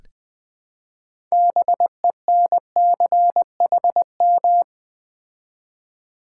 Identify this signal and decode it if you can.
Morse code, heard as a keyed tone, about 20 wpm, 700 Hz. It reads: BENCHM
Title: Build A Word - 3 Letters At A Time - 9-Letter Words - 20wpm